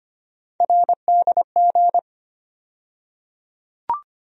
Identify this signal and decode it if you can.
Morse code, heard as a keyed tone, about 25 wpm, 700 Hz. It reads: RDG
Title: USA Call Sign Postfixes - 3 Letters 25wpm